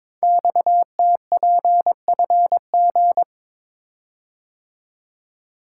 Morse code, heard as a keyed tone, about 22 wpm, 700 Hz. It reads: XTPFG
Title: Code Groups - 5 Letters - 22wpm